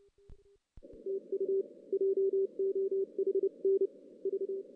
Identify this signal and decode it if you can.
Morse code, heard as a keyed tone, about 29 wpm, 380 Hz. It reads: TU JOHN V
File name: 2016 JAN CW NAQP / 2016 Jan naqp k5ot - 0-6